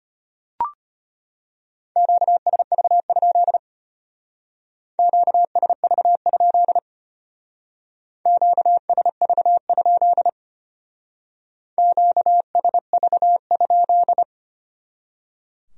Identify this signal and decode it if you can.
Morse code, heard as a keyed tone, about 30 wpm, 700 Hz. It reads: QSV? QSV? QSV? QSV?
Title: Speed-Racing Q-Codes - 25wpm